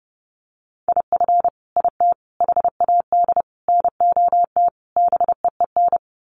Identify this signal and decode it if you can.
Morse code, heard as a keyed tone, about 30 wpm, 700 Hz. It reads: IF IT HAD NOT BEEN
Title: Sets of 5 Words Encore 30wpm